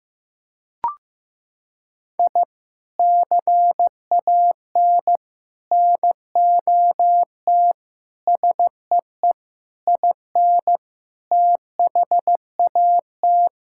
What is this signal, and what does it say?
Morse code, heard as a keyed tone, about 16 wpm, 700 Hz. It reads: I CAN NOT SEE IN THAT
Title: Sentences from Top 1000 Words - Review Part 5 - 15wpm